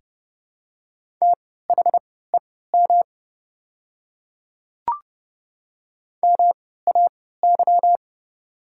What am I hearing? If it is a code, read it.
Morse code, heard as a keyed tone, about 30 wpm, spaced as for 17 wpm, 700 Hz. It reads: THEM MAY
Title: 3x Character Spacing - Top 100 Words - 30wpm